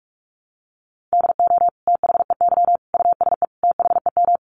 Morse code, heard as a keyed tone, about 44 wpm, 700 Hz. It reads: DO THEY USE THEM
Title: Sentences from Top 1300 Words 45wpm